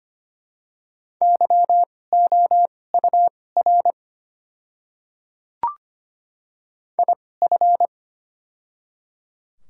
Morse code, heard as a keyed tone, about 25 wpm, spaced as for 18 wpm, 690 Hz. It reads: YOUR IF
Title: 2x Character Spacing - Top 100 Words - 25wpm